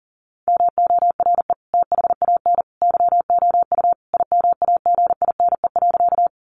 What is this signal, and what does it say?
Morse code, heard as a keyed tone, about 37 wpm, 700 Hz. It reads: MORE THAN YOU IMAGINE.
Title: Sentences from Top 1400 Words 40wpm